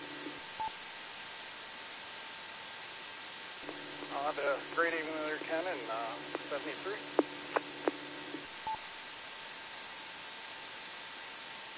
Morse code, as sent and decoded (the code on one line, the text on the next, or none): none